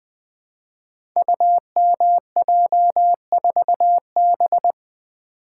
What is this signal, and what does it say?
Morse code, heard as a keyed tone, about 20 wpm, 700 Hz. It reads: UMJ4B
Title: Code Groups - 5 Letter:Numbers - 20wpm